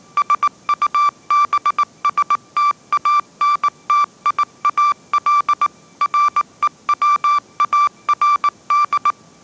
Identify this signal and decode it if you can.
Morse code, heard as a keyed tone, about 19 wpm, 1170 Hz. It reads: SUBSTANTIAL REWARD
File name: 20wpm